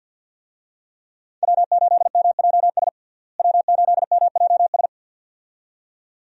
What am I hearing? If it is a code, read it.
Morse code, heard as a keyed tone, about 48 wpm, 700 Hz. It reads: W8MJS W8MJS
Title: USA Call Signs - 5 Characters - Repeated 2x - 50wpm